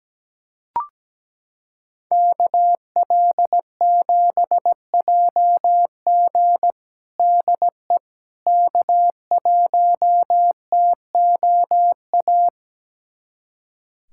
Morse code, heard as a keyed tone, about 17 wpm, 700 Hz. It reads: KL7JG DE K1TOA
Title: USA Callsign DE Callsign - 5 Characters - 17wpm